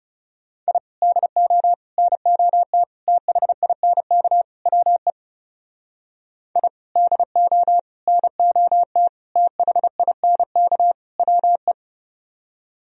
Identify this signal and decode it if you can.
Morse code, heard as a keyed tone, about 40 wpm, 700 Hz. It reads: I DO NOT THINK WE I DO NOT THINK WE